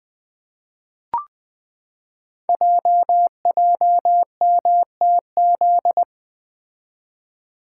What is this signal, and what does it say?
Morse code, heard as a keyed tone, about 20 wpm, 700 Hz. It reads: JJMTZ
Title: Code Groups - 5 Letters - 20wpm